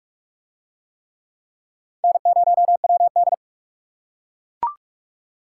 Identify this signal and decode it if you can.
Morse code, heard as a keyed tone, about 41 wpm, 700 Hz. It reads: N0WD